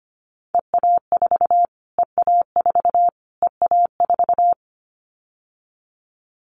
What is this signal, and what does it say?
Morse code, heard as a keyed tone, about 25 wpm, 700 Hz. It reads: EA4 EA4 EA4
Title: Callsign Prefixes for Top 25 DXCC - Repeated 3x - 25wpm